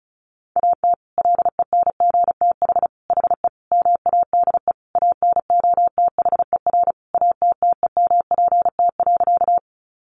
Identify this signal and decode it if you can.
Morse code, heard as a keyed tone, about 34 wpm, 700 Hz. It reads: AT LENGTH HE MADE ANOTHER ATTEMPT.
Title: Sentences from Top 1000 Words 35wpm